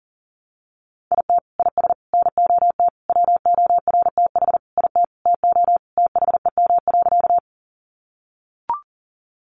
Morse code, heard as a keyed tone, about 40 wpm, 700 Hz. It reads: IT IS NOT WORTH IT TO THEM.
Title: Sentences from Top 800 Words 40wpm